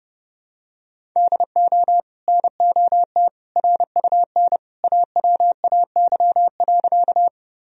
Morse code, heard as a keyed tone, about 32 wpm, 700 Hz. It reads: DO NOT RUN AWAY.
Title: Sentences from Top 1000 Words - Review Part 1 - 30wpm